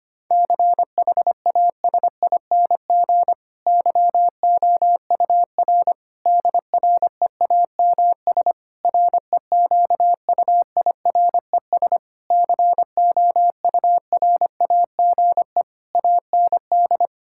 Morse code, heard as a keyed tone, about 25 wpm, 700 Hz. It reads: CHASING YOUR DREAMS REQUIRES COURAGE AND
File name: practice / 25wpm / Difficult Sentences with Hints - 25wpm